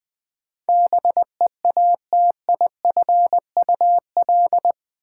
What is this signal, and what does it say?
Morse code, heard as a keyed tone, about 20 wpm, 700 Hz. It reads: BEATIFUL
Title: Sentences from Top 1300 Words - New Word Drill - 20wpm